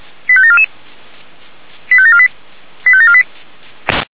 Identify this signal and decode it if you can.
Five-tone selective call (ZVEI), heard as quick runs of five tones, tones of about 70 ms. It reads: 85744 85748 85748
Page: SDR/ZVEI